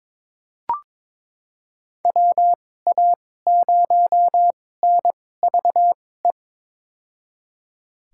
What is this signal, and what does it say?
Morse code, heard as a keyed tone, about 21 wpm, spaced as for 16 wpm, 700 Hz. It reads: WA0NVE
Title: USA Call Signs - 6 Characters - 2x Character Spacing - 22wpm